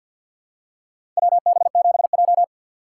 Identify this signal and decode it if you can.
Morse code, heard as a keyed tone, about 53 wpm, 700 Hz. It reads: WB7J
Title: USA Call Signs - 4 Characters - 6x Character Spacing - 50wpm